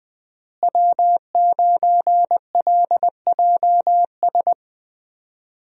Morse code, heard as a keyed tone, about 20 wpm, 700 Hz. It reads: W9LJS
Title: USA Call Signs - 5 Characters - Repeated 3x - 20wpm